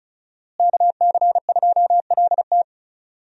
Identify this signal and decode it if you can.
Morse code, heard as a keyed tone, about 34 wpm, 700 Hz. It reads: KC2LT
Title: USA Call Signs - 5 Characters - Repeated 2x - 35wpm